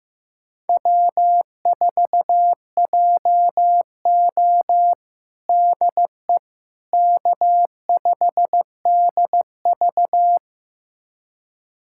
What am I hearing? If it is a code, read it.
Morse code, heard as a keyed tone, about 16 wpm, 700 Hz. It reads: W4JO DE K5DV
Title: USA Callsign DE Callsign - 4 Characters - 15wpm